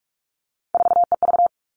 Morse code, heard as a keyed tone, about 44 wpm, 700 Hz. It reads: 4EV
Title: All Cut Numbers - 3-Digit 45wpm